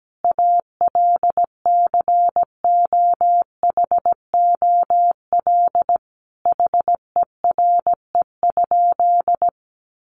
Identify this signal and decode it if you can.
Morse code, heard as a keyed tone, about 17 wpm, 700 Hz. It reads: ALCOHOL HERE?
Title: Sentences from Top 1400 Words 17wpm